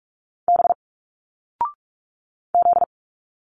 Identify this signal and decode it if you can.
Morse code, heard as a keyed tone, about 46 wpm, 700 Hz. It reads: B Z
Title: Single Letters 45wpm